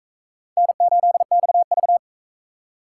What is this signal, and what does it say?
Morse code, heard as a keyed tone, about 44 wpm, 700 Hz. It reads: N8XV